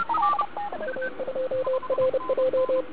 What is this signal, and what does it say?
Morse code, heard as a keyed tone, about 30 wpm, 530 Hz. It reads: A2RJ
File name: ARCHIV / 40m cw1